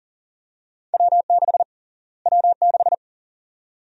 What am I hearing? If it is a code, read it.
Morse code, heard as a keyed tone, about 40 wpm, 700 Hz. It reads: W6 W6
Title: USA Call Sign Prefixes - Repeated 2x - 40wpm